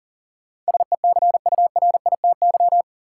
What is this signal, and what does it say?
Morse code, heard as a keyed tone, about 40 wpm, 700 Hz. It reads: SECURITY
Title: Sentences from Top 1400 Words - New Word Drill Alt - 40wpm